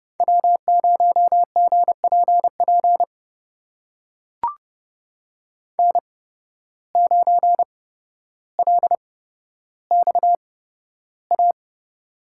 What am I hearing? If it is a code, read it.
Morse code, heard as a keyed tone, about 30 wpm, spaced as for 13 wpm, 700 Hz. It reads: W0GPP N9LXA
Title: USA Call Signs - 5 Characters - 8x Character Spacing - 30wpm